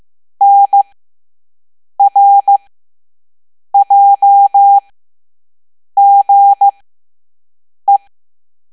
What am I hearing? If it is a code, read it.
Morse code, heard as a keyed tone, about 15 wpm, spaced as for 6 wpm, 800 Hz. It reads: NRJGE
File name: Morse / Level7-All-A